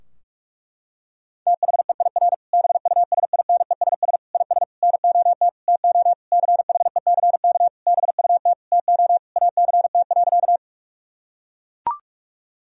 Morse code, heard as a keyed tone, about 41 wpm, 700 Hz. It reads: THEIR BUSINESS IS NOT TO CHECK BUT TO ACT.
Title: Sentences from Top 1000 Words - Review Part 3 - 45wpm